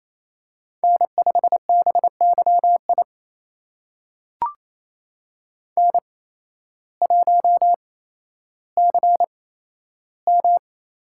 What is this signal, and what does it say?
Morse code, heard as a keyed tone, about 28 wpm, spaced as for 13 wpm, 700 Hz. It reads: N5BYI N1CM
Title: USA Call Signs - 5 Characters - 8x Character Spacing - 28wpm